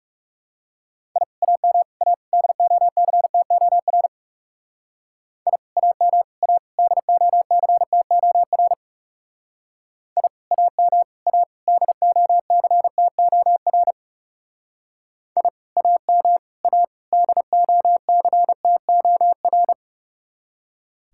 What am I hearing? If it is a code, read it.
Morse code, heard as a keyed tone, about 34 wpm, 700 Hz. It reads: I AM A DOCTOR I AM A DOCTOR I AM A DOCTOR I AM A DOCTOR
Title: Speed-Racing Common QSO Phrases Encore - 30wpm